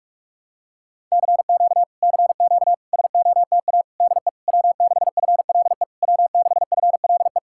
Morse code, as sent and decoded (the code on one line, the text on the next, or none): -.-. --.- / -.-. --.- / ... --- - .- / -.. / .-- -.... ..-. .-.. / .-- -.... ..-. .-.. .
CQ CQ SOTA DE W6FLE W6FLE